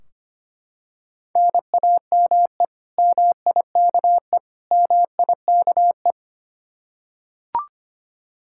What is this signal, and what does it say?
Morse code, heard as a keyed tone, about 25 wpm, 700 Hz. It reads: NAME MIKE MIKE
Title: Practice QSO - Exchange 1 - Beginning - 25wpm